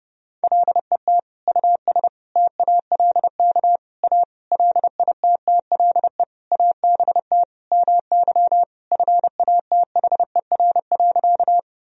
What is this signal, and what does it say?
Morse code, heard as a keyed tone, about 30 wpm, 700 Hz. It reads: LET US TALK A LITTLE ABT MY FATHER.
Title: Sentences from Top 400 Words and Abbreviations 30wpm